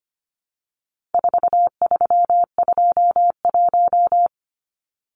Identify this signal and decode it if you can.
Morse code, heard as a keyed tone, about 25 wpm, 700 Hz. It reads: 4321
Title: Numbers - 4-Digit 25wpm